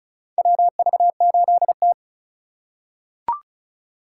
Morse code, heard as a keyed tone, about 34 wpm, 690 Hz. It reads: WV8T
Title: USA Call Signs - 3x Character Spacing - 35wpm